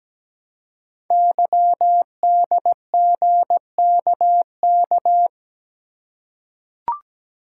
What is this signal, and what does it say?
Morse code, heard as a keyed tone, about 17 wpm, 700 Hz. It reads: YDGKK
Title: Code Groups - 5 Letters - 17wpm